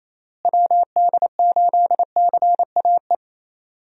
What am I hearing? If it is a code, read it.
Morse code, heard as a keyed tone, about 28 wpm, 700 Hz. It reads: WD8CAE
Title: USA Call Signs - 6 Characters - Repeated 3x - 28wpm